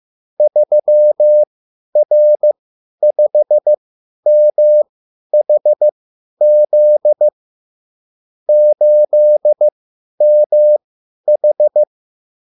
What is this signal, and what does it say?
Morse code, heard as a keyed tone, about 15 wpm, spaced as for 10 wpm, 600 Hz. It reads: 3R5MHZ 8MH
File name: QSL / CW lesson